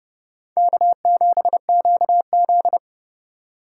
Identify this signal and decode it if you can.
Morse code, heard as a keyed tone, about 39 wpm, 700 Hz. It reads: K7QZ